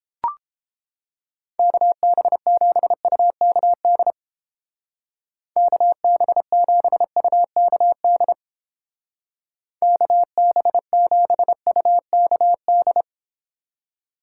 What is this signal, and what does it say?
Morse code, heard as a keyed tone, about 31 wpm, 700 Hz. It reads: KB7UKD KB7UKD KB7UKD